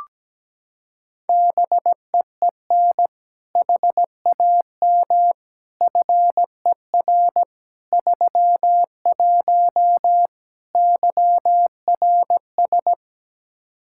Morse code, heard as a keyed tone, about 17 wpm, 700 Hz. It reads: BEEN HAM FER 31 YRS
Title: Practice QSO - Exchange 3 - More Info - 17wpm